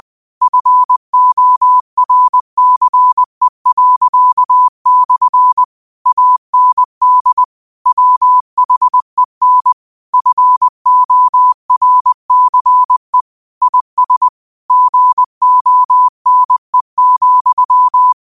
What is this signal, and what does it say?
Morse code, heard as a keyed tone, about 20 wpm, 1000 Hz. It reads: FORCE./ AND WHEN FORCE IS GONE,